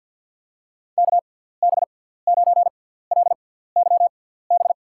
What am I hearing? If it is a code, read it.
Morse code, heard as a keyed tone, about 53 wpm, spaced as for 17 wpm, 700 Hz. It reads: KB9LYB
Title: USA Call Signs - 6 Characters - 6x Character Spacing - 50wpm